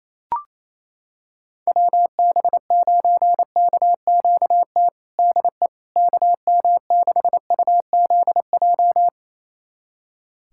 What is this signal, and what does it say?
Morse code, heard as a keyed tone, about 28 wpm, 700 Hz. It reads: WB9KQT DE KM6UZJ
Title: USA Callsign DE Callsign - 6 Characters - 28wpm